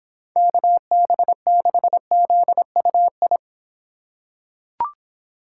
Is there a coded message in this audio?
Yes: Morse code from 0.4 to 3.4 s, heard as a keyed tone, about 25 wpm, 700 Hz. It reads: KB6ZUI